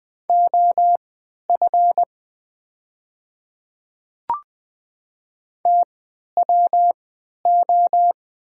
Morse code, heard as a keyed tone, about 20 wpm, spaced as for 11 wpm, 700 Hz. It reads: OF TWO
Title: 3x Character Spacing - Top 100 Words - 20wpm